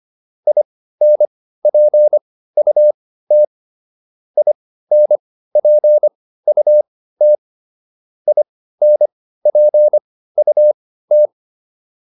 Morse code, heard as a keyed tone, about 25 wpm, spaced as for 15 wpm, 600 Hz. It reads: INPUT INPUT INPUT
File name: IN5 15